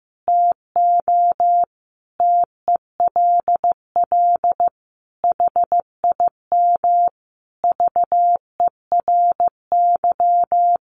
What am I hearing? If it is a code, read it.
Morse code, heard as a keyed tone, about 15 wpm, 700 Hz. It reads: TO TELL HIM VERY
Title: Sentences from Top 800 Words 15wpm